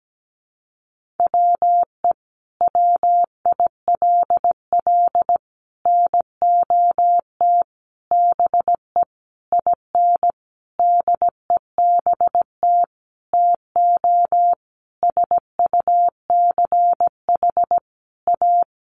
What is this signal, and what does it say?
Morse code, heard as a keyed tone, about 17 wpm, 700 Hz. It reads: WE WILL NOT BE IN DEBT TO SUCH A
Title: Sentences from Top 1300 Words 17wpm